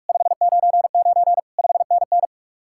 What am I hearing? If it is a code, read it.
Morse code, heard as a keyed tone, about 58 wpm, 700 Hz. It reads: 599 5NN